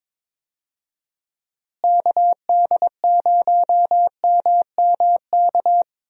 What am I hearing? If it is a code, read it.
Morse code, heard as a keyed tone, about 22 wpm, 700 Hz. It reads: KD0MMK
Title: USA Call Signs - 6 Characters - Repeated 3x - 22wpm